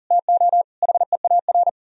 Morse code, heard as a keyed tone, about 38 wpm, 700 Hz. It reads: TO HEAR
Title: Speed-Racing - Sets of 5 Words 40wpm